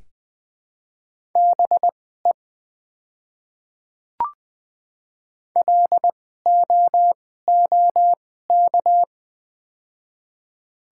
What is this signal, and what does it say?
Morse code, heard as a keyed tone, about 20 wpm, spaced as for 14 wpm, 700 Hz. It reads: BE LOOK
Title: 2x Character Spacing - Top 100 Words - 20wpm